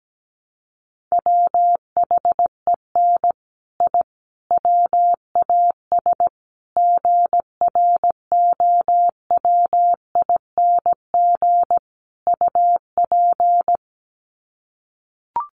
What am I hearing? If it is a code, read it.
Morse code, heard as a keyed tone, about 17 wpm, 700 Hz. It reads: WHEN I WAS GROWING UP
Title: Sets of 5 Words Encore 17wpm